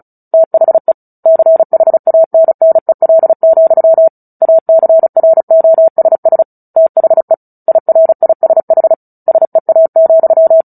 Morse code, heard as a keyed tone, about 34 wpm, 650 Hz. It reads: THE CHANNEL, ACROSS THE IRISH SEA,
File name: Book Two / 09-Chapter 9 - 35 WPM